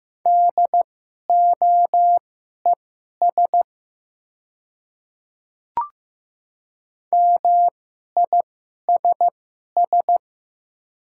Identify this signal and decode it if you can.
Morse code, heard as a keyed tone, about 15 wpm, spaced as for 11 wpm, 700 Hz. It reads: DOES MISS
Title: 2x Character Spacing - Top 500 Words - 15wpm